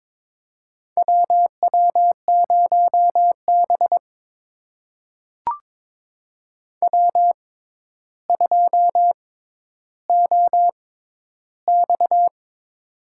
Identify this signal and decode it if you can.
Morse code, heard as a keyed tone, about 22 wpm, spaced as for 11 wpm, 700 Hz. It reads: WW0B W2OX
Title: USA Call Signs - 4 Characters - 6x Character Spacing - 22wpm